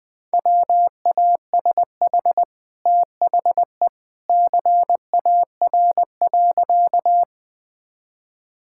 Morse code, heard as a keyed tone, about 21 wpm, 700 Hz. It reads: WASH THE CAR.
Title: Easy Sentences with Hints - 20wpm